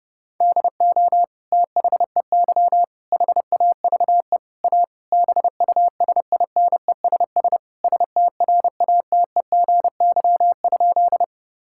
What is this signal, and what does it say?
Morse code, heard as a keyed tone, about 30 wpm, 700 Hz. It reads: DO THEY HAVE A BUSINESS STRATEGY?
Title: Sentences from Top 1000 Words - Review Part 5 - 30wpm